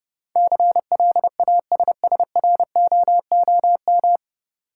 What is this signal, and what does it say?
Morse code, heard as a keyed tone, about 30 wpm, 700 Hz. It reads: CLASSROOM
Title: Sentences from Top 1500 Words - New Word Drill - 30wpm